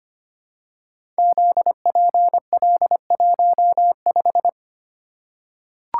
Morse code, heard as a keyed tone, about 25 wpm, 700 Hz. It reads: ZPL15
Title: Code Groups - 5 Letter:Numbers - 25wpm